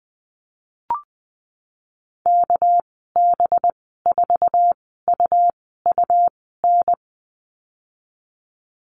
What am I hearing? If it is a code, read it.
Morse code, heard as a keyed tone, about 19 wpm, spaced as for 14 wpm, 700 Hz. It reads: KB4UUN
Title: 207 Lesson 33g - Letter j - Call Signs 20wpm